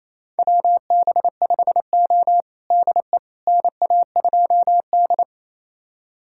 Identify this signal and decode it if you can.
Morse code, heard as a keyed tone, about 28 wpm, 690 Hz. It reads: WB5O DE NA2D
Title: USA Callsign DE Callsign - 4 Characters - 28wpm